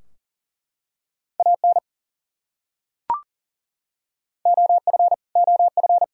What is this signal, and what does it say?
Morse code, heard as a keyed tone, about 40 wpm, 700 Hz. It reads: AN OF OF
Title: Top 100 Words Repeated 40wpm